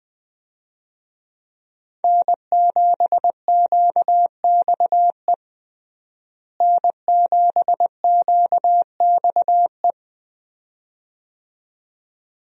Morse code, heard as a keyed tone, about 20 wpm, 700 Hz. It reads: N7QXE N7QXE
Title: USA Call Signs - 5 Characters - Repeated 2x - 20wpm